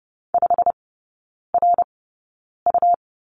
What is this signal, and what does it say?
Morse code, heard as a keyed tone, about 30 wpm, spaced as for 9 wpm, 700 Hz. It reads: 5RU
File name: Lesson 17e - Letter u - Three Characters 30wpm